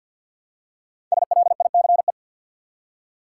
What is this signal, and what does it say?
Morse code, heard as a keyed tone, about 52 wpm, 700 Hz. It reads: SLICE
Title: Five Letter Words - 50wpm